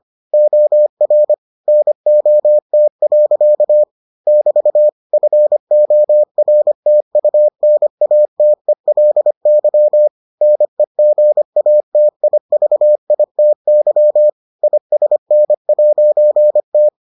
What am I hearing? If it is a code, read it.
Morse code, heard as a keyed tone, about 25 wpm, 600 Hz. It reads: OR NOT. <BT> FORTUNATELY NEGATIVITY ISN'T